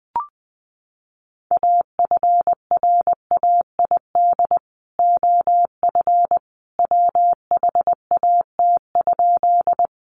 Morse code, heard as a keyed tone, about 20 wpm, 700 Hz. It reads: AFRAID OF WHAT?
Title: Sentences from Top 1500 Words 20wpm